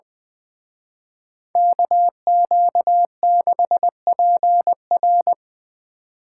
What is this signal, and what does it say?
Morse code, heard as a keyed tone, about 24 wpm, 700 Hz. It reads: KQ6PR
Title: Speed Racing - USA 2x2 Call Signs 20wpm